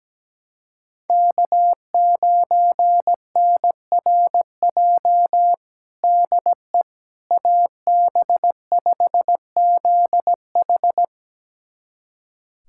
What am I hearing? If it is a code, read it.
Morse code, heard as a keyed tone, about 17 wpm, 700 Hz. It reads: K9NRJ DE AB5ZH
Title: USA Callsign DE Callsign - 5 Characters - 17wpm